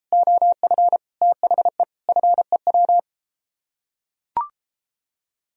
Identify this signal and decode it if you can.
Morse code, heard as a keyed tone, about 30 wpm, 700 Hz. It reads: OF THE FEW